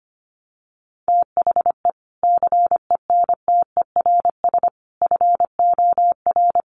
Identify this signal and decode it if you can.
Morse code, heard as a keyed tone, about 25 wpm, 700 Hz. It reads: THE CENTERS FOR
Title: Sets of 5 Words Encore2 25wpm